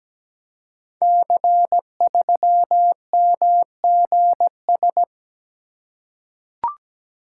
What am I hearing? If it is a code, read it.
Morse code, heard as a keyed tone, about 17 wpm, 700 Hz. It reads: C3MGS
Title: Code Groups - 5 Letter:Numbers - 17wpm